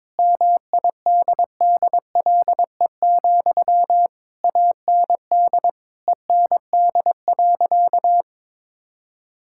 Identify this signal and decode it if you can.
Morse code, heard as a keyed tone, about 22 wpm, 700 Hz. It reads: MIDDLE, AND END.
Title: Sentences from Top 1000 Words - Review Part 6 - 22wpm